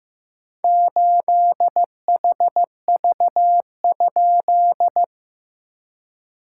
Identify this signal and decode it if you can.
Morse code, heard as a keyed tone, about 15 wpm, 700 Hz. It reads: 8HV?
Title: SOTA QSO Practice 15wpm